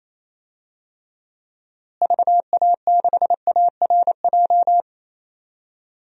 Morse code, heard as a keyed tone, about 28 wpm, 700 Hz. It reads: VA6ARJ
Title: DXCC-100 Call Signs - 6 Characters - Repeated 2x - 28wpm